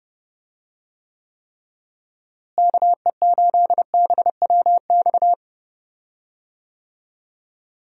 Morse code, heard as a keyed tone, about 30 wpm, 700 Hz. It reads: KE8BWX
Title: USA Call Signs - 6 Characters - Repeated 3x - 30wpm